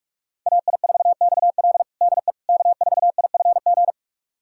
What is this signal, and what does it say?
Morse code, heard as a keyed tone, about 46 wpm, 700 Hz. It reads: AI4XL DE K4IFG
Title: USA Callsign DE Callsign - 5 Characters - 45wpm